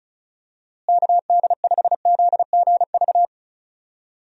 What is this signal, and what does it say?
Morse code, heard as a keyed tone, about 36 wpm, 700 Hz. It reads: KD5ZGV
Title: USA Call Signs - 6 Characters - Repeated 2x - 35wpm